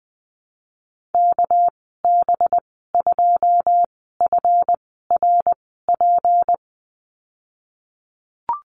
Morse code, heard as a keyed tone, about 20 wpm, 700 Hz. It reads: K B 2 F R P
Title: 238 Lesson 38g - Letter q - Call Signs 20wpm